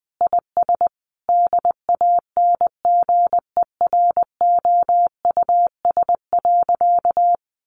Morse code, heard as a keyed tone, about 20 wpm, 700 Hz. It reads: IS DANGEROUS.
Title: Sentences from Top 1200 Words 20wpm